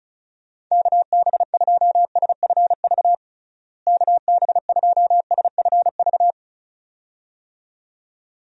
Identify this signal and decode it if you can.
Morse code, heard as a keyed tone, about 36 wpm, 700 Hz. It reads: KB2SFV KB2SFV
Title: USA Call Signs - 6 Characters - Repeated 2x - 35wpm